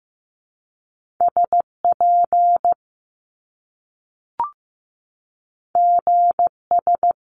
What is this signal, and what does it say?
Morse code, heard as a keyed tone, about 15 wpm, 700 Hz. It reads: SP GS
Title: Top Two Letter Combinations in Words 15wpm